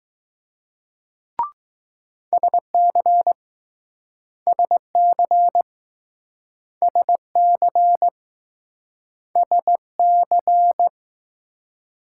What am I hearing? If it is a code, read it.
Morse code, heard as a keyed tone, about 20 wpm, 700 Hz. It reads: SC SC SC S C